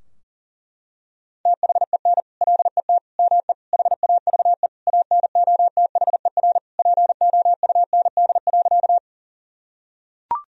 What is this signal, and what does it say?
Morse code, heard as a keyed tone, about 37 wpm, 700 Hz. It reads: THEN LET ME HAVE ANOTHER POUND.
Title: Sentences from Top 1000 Words - Review Part 2 - 40wpm